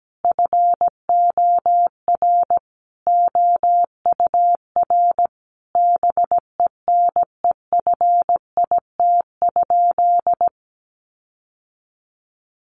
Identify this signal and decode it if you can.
Morse code, heard as a keyed tone, about 17 wpm, 700 Hz. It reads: FOR OUR BENEFIT?
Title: Sentences from Top 1500 Words 17wpm